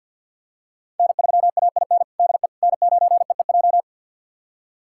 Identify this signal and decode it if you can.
Morse code, heard as a keyed tone, about 50 wpm, 700 Hz. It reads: N3AIN DE N9EEJ
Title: USA Callsign DE Callsign 50wpm